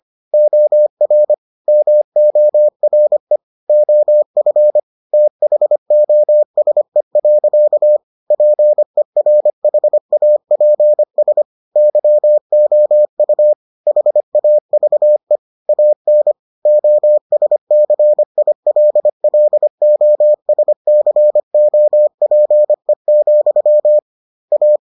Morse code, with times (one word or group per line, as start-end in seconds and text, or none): 0.3-1.3 OR
1.7-3.4 MORE
3.7-4.8 OF
5.1-8.0 THOSE.
8.3-11.4 PERHAPS
11.8-13.5 YOU
13.9-15.4 HAVE
15.7-16.3 AN
16.7-24.0 OSCILLOSCOPE,
24.5-24.8 A